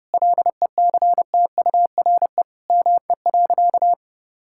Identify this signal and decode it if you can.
Morse code, heard as a keyed tone, about 30 wpm, 700 Hz. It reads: LECTURE ME.
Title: Sentences from Top 1600 Words 30wpm